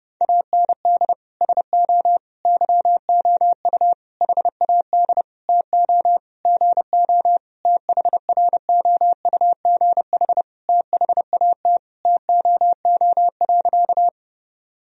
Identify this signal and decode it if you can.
Morse code, heard as a keyed tone, about 32 wpm, 700 Hz. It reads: AND SO YOU HAD TO GO THROUGH THAT TOO.
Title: Sentences from Top 1000 Words - Review Part 1 - 30wpm